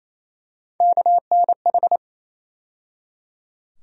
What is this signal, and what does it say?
Morse code, heard as a keyed tone, about 27 wpm, 700 Hz. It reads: KNH
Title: USA Call Sign Postfixes - 3 Letters 28wpm